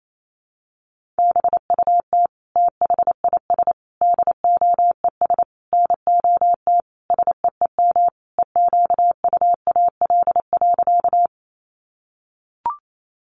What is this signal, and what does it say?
Morse code, heard as a keyed tone, about 28 wpm, 700 Hz. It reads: BUT THIS DOES NOT SEEM EQUAL.
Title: Sentences from Top 1400 Words 28wpm